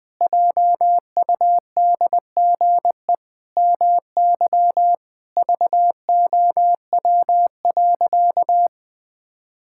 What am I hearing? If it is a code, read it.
Morse code, heard as a keyed tone, about 20 wpm, 700 Hz. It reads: JUDGE MY VOW.
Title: Pangram Challenge v2 - 20wpm